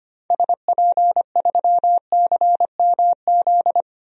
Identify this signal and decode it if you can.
Morse code, heard as a keyed tone, about 25 wpm, 700 Hz. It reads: SP3CMZ
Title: DXCC-100 Call Signs - 6 Characters - Repeated 2x - 25wpm